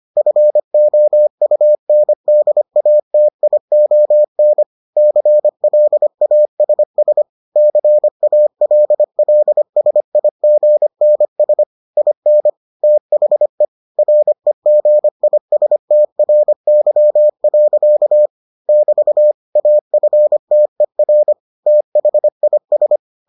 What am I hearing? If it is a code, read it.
Morse code, heard as a keyed tone, about 25 wpm, 600 Hz. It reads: FOUNDATION CLASS CALLSIGNS IN THE REGISTRY. <BT> AFTER THIS